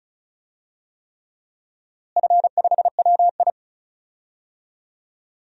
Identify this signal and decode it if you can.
Morse code, heard as a keyed tone, about 36 wpm, 700 Hz. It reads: F5WI
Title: DXCC-100 Call Signs - 4 Characters - Repeated 3x - 35wpm